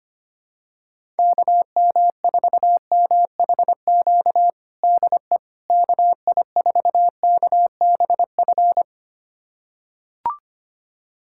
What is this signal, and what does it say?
Morse code, heard as a keyed tone, about 25 wpm, 700 Hz. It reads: KM4MHQ DE KI4KBF
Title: USA Callsign DE Callsign - 6 Characters - 25wpm